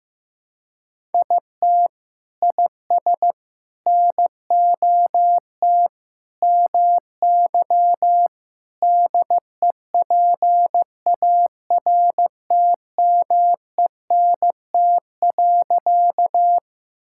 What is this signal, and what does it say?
Morse code, heard as a keyed tone, about 15 wpm, 700 Hz. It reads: IT IS NOT MY DEPARTMENT.
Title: Sentences from Top 1000 Words - Review Part 5 - 15wpm